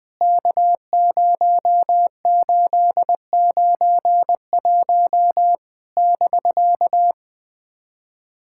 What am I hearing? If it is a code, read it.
Morse code, heard as a keyed tone, about 20 wpm, 700 Hz. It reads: K0891 <BK>
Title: POTA QSO Practice - 20wpm